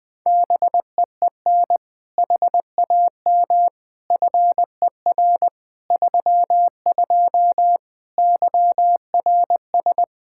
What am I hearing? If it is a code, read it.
Morse code, heard as a keyed tone, about 20 wpm, 700 Hz. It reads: BEEN HAM FER 32 YRS
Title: K1USN SST QSO Practice 20wpm